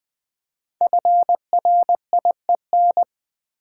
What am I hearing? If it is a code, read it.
Morse code, heard as a keyed tone, about 20 wpm, 700 Hz. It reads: FRIEN
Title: Build A Word - 5 Letters At A Time - 10-Letter Words - 20wpm